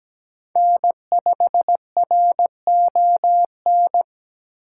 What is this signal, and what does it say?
Morse code, heard as a keyed tone, about 20 wpm, 700 Hz. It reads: N5RON
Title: Speed Racing - USA 1x3 Call Signs 17wpm